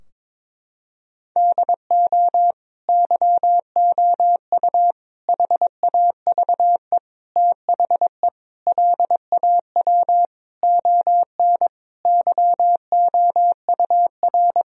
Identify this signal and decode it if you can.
Morse code, heard as a keyed tone, about 22 wpm, 700 Hz. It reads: DO YOU HAVE THE LAW ON YOUR
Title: Sentences from Top 1000 Words - Review Part 3 - 22wpm